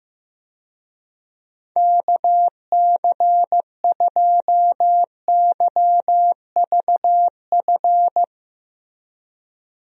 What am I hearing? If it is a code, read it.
Morse code, heard as a keyed tone, about 15 wpm, 700 Hz. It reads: KC2YVF
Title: USA Call Signs - 6 Characters - 15wpm